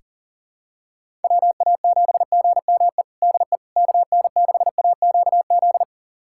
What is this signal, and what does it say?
Morse code, heard as a keyed tone, about 40 wpm, 700 Hz. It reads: WA7GME DE KN6AQZ
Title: USA Callsign DE Callsign 40wpm